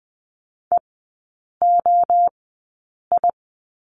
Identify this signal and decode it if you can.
Morse code, heard as a keyed tone, about 20 wpm, spaced as for 8 wpm, 700 Hz. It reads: EOI
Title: 053 Lesson 10e - Letter r - Three Characters 20wpm